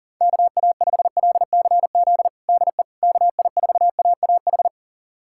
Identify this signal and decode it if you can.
Morse code, heard as a keyed tone, about 40 wpm, 700 Hz. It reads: KA5LCZ DE KI4AAH
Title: USA Callsign DE Callsign - 6 Characters - 40wpm